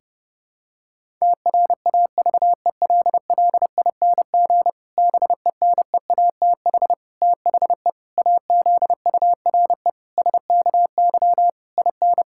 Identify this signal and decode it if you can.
Morse code, heard as a keyed tone, about 30 wpm, 700 Hz. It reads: TRAVELLING BENEATH THE AZURE SKY IN
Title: Pangram Challenge v2 - 30wpm